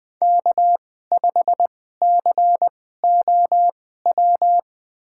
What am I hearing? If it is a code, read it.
Morse code, heard as a keyed tone, about 20 wpm, spaced as for 14 wpm, 700 Hz. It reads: K5COW
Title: USA Call Signs - 2x Character Spacing - 20wpm